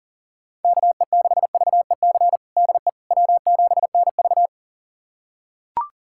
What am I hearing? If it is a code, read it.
Morse code, heard as a keyed tone, about 40 wpm, 700 Hz. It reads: KE6VEC DE W7NV
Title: USA Callsign DE Callsign 40wpm